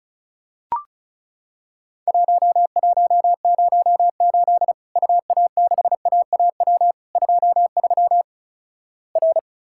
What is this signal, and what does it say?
Morse code, heard as a keyed tone, about 36 wpm, 700 Hz. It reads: 1108 UA6AAW 23 R
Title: Worked All Europe DX Contest - QTC Exchange - 35wpm